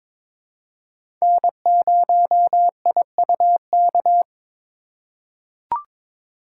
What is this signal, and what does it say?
Morse code, heard as a keyed tone, about 23 wpm, 700 Hz. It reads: N0IUK
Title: USA Call Signs - 5 Characters - 2x Character Spacing - 22wpm